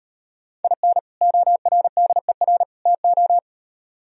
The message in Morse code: IN ORDER TO

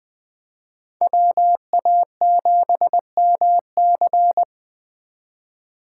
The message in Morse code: WA7MC

